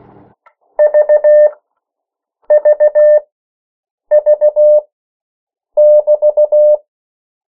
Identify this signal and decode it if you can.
Morse code, heard as a keyed tone, about 16 wpm, spaced as for 7 wpm, 600 Hz. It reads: VVV<BT>
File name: morse mp3 / 14 urok 13